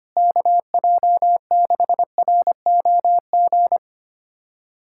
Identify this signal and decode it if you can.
Morse code, heard as a keyed tone, about 25 wpm, 700 Hz. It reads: KJ6ROG
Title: USA Callsign DE Callsign - 6 Characters - 25wpm